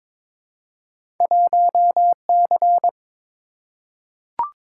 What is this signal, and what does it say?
Morse code, heard as a keyed tone, about 22 wpm, 700 Hz. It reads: 1C
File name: practice / 22wpm / Two Letter-Number Combinations - 22wpm